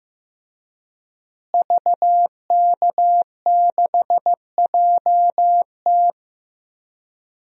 Morse code, heard as a keyed tone, about 15 wpm, 700 Hz. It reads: VK6JT
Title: DXCC-100 Call Signs - 5 Characters - Repeated 2x - 15wpm